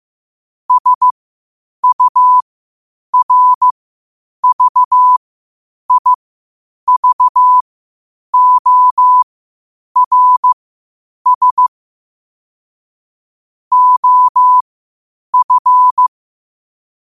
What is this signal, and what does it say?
Morse code, heard as a keyed tone, about 15 wpm, spaced as for 8 wpm, 1000 Hz. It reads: SURVIVORS OF
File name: 8wpm